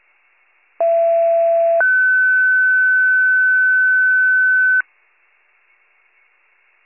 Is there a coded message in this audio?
Yes: two-tone page from 0.8 to 4.8 s, heard as a one-second tone, then a three-second tone, about 670 then 1500 Hz.